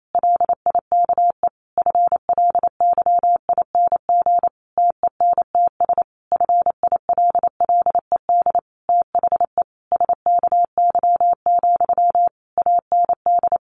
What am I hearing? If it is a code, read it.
Morse code, heard as a keyed tone, about 28 wpm, 700 Hz. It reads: LIKE FLYING TENTS FILLED THE SKY, AND
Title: A Sound Of Thunder 28wpm